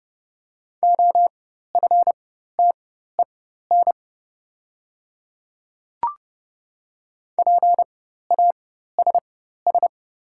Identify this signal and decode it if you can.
Morse code, heard as a keyed tone, about 30 wpm, spaced as for 14 wpm, 690 Hz. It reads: OFTEN PASS
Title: 4x Character Spacing - Top 500 Words - 30wpm